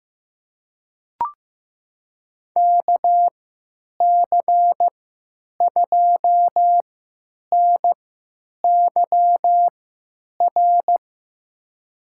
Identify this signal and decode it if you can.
Morse code, heard as a keyed tone, about 15 wpm, spaced as for 9 wpm, 700 Hz. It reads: KC2NYR
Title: USA Call Signs - 6 Characters - 3x Character Spacing - 15wpm